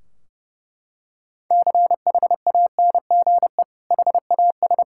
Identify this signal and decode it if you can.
Morse code, heard as a keyed tone, about 30 wpm, 700 Hz. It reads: CHANGE HAS
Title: Sentences from Top 1600 Words 30wpm